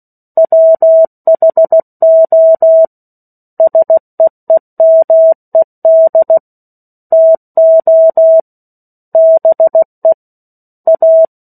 Morse code, heard as a keyed tone, about 16 wpm, 650 Hz. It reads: WHO SEEMED TO BE A
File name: 16 WPM